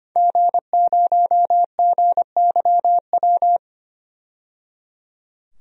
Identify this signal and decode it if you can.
Morse code, heard as a keyed tone, about 25 wpm, 700 Hz. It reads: G0GYW
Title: DXCC-100 Call Signs - 5 Characters - 25wpm